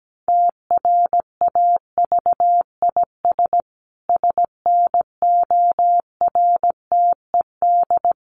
Morse code, heard as a keyed tone, about 17 wpm, 700 Hz. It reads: TRAVIS SNORTED
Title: A Sound Of Thunder 17wpm